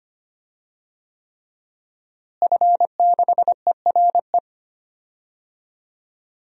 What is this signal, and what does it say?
Morse code, heard as a keyed tone, about 25 wpm, 700 Hz. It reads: F6ERE
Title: DXCC-100 Call Signs - 5 Characters - Repeated 3x - 25wpm